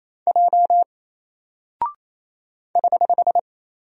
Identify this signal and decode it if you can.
Morse code, heard as a keyed tone, about 27 wpm, 700 Hz. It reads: J <HH>
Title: Single Character (Letter:Number:Punctuation:Prosign) 28wpm